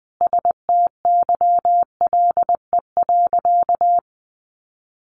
Morse code, heard as a keyed tone, about 20 wpm, 700 Hz. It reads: STYLE.
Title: Sentences from Top 800 Words 20wpm